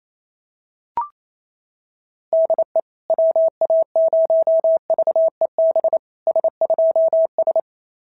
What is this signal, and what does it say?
Morse code, heard as a keyed tone, about 27 wpm, 690 Hz. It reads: DE WA0VEB S2S
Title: SOTA QSO Practice 28wpm